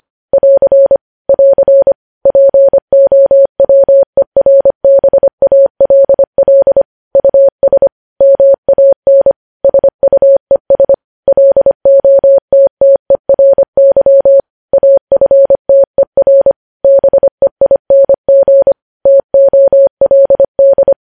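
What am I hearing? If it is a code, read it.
Morse code, heard as a keyed tone, about 25 wpm, 550 Hz. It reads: <AR> <AR> POWERBALL US MAN SUES LOTTERY AFTER BEING TOLD
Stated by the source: Morse code practice of news headlines sent at 25 wpm.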